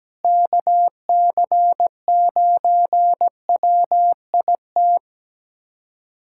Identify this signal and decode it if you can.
Morse code, heard as a keyed tone, about 17 wpm, 700 Hz. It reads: KC9WIT
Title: USA Callsign DE Callsign 17wpm